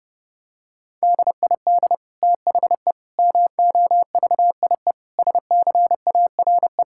Morse code, heard as a keyed tone, about 32 wpm, 700 Hz. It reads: DID THE MOVIE SCARE
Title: Sentences from Top 1000 Words - Review Part 5 - 30wpm